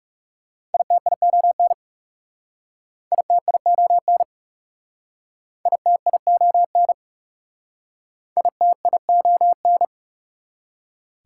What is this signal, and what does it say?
Morse code, heard as a keyed tone, about 39 wpm, 700 Hz. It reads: ITION ITION ITION ITION